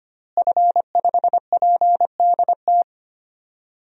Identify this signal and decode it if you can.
Morse code, heard as a keyed tone, about 25 wpm, 700 Hz. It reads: F5PDT
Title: DXCC-100 Call Signs - 5 Characters - Repeated 2x - 25wpm